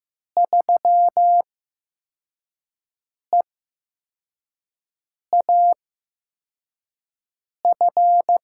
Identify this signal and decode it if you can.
Morse code, heard as a keyed tone, about 15 wpm, spaced as for 4 wpm, 700 Hz. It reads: 3EAF
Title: USA Call Signs - 5 Characters - 8x Character Spacing - 15wpm